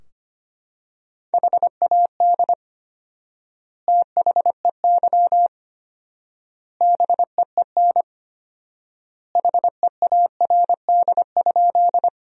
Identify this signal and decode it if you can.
Morse code, heard as a keyed tone, about 25 wpm, 700 Hz. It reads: HAD THEY BEEN HEARD?
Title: 4x Word-Spacing - Sentences from Top 400 Words - 25wpm